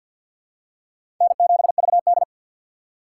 Morse code, heard as a keyed tone, about 51 wpm, 700 Hz. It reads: N7VD
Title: USA Call Signs - 4 Characters - 2x Character Spacing - 50wpm